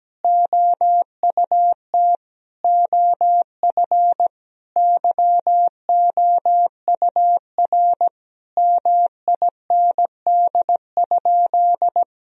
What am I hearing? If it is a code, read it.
Morse code, heard as a keyed tone, about 17 wpm, 700 Hz. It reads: OUT OF YOUR MIND?
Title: Sentences from Top 1000 Words - Review Part 4 - 17wpm